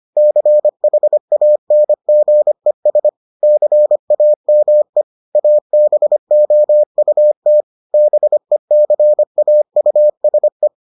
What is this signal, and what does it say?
Morse code, heard as a keyed tone, about 25 wpm, 600 Hz. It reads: CHANGES CAME ABOUT BECAUSE